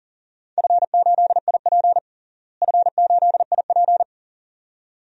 Morse code, heard as a keyed tone, about 40 wpm, 690 Hz. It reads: F8IP F8IP
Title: DXCC-100 Call Signs - 4 Characters - Repeated 2x - 40wpm